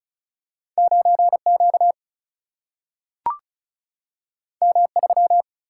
Morse code, heard as a keyed tone, about 36 wpm, 700 Hz. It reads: 9 Q M 3
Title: Single-Letter Single-Number 35wpm